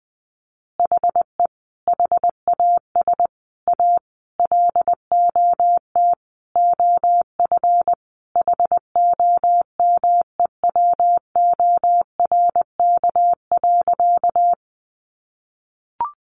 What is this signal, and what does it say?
Morse code, heard as a keyed tone, about 20 wpm, 700 Hz. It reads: HE HAS A LOT OF HOMEWORK.
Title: Additional Easy Phrases 20wpm